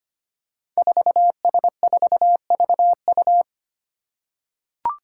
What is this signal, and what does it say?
Morse code, heard as a keyed tone, about 25 wpm, 700 Hz. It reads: 4S4VU
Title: Five Challenging Letter-Number Combinations 25wpm